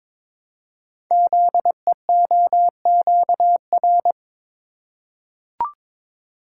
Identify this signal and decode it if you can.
Morse code, heard as a keyed tone, about 22 wpm, 700 Hz. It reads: ZEOQR
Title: Code Groups - 5 Letters - 22wpm